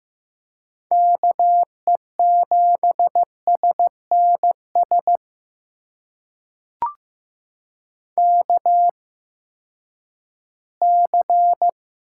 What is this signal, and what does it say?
Morse code, heard as a keyed tone, about 15 wpm, spaced as for 10 wpm, 700 Hz. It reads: KE7SNS KC
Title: USA Call Signs - 6 Characters - 8x Character Spacing - 15wpm